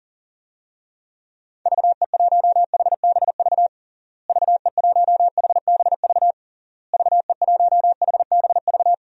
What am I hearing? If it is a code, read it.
Morse code, heard as a keyed tone, about 37 wpm, 700 Hz. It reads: VE1HBV VE1HBV VE1HBV
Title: DXCC-100 Call Signs - 6 Characters - Repeated 3x - 40wpm